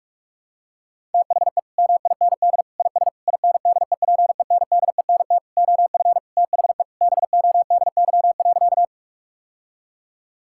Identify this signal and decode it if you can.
Morse code, heard as a keyed tone, about 46 wpm, 700 Hz. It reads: THE MIND IS INDEPENDENT OF THE BODY.
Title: Sentences from Top 1000 Words - Review Part 6 - 45wpm